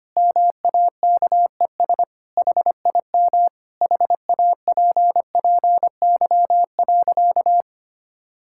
Morse code, heard as a keyed tone, about 25 wpm, 700 Hz. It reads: MAKES HIM HAPPY.
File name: practice / 25wpm / Sentences with 2-Word Phrases - 25wpm